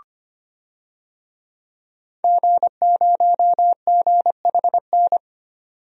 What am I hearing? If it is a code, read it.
Morse code, heard as a keyed tone, about 25 wpm, 700 Hz. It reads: G0GHN
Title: DXCC-100 Call Signs - 5 Characters - 25wpm